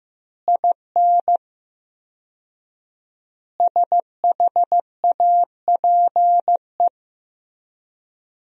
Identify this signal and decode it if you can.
Morse code, heard as a keyed tone, about 16 wpm, 700 Hz. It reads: IN SHAPE
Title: 4x Word-Spacing - Sentences from Top 500 Words - 15wpm